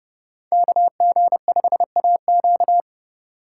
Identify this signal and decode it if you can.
Morse code, heard as a keyed tone, about 30 wpm, 700 Hz. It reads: KG5AQ